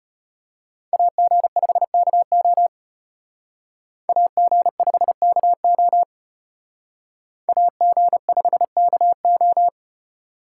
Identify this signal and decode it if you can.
Morse code, heard as a keyed tone, about 32 wpm, 700 Hz. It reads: AG5KO AG5KO AG5KO